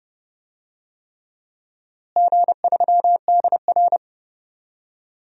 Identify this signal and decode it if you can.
Morse code, heard as a keyed tone, about 31 wpm, 700 Hz. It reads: G3DR
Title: DXCC-100 Call Signs - 4 Characters - Repeated 3x - 30wpm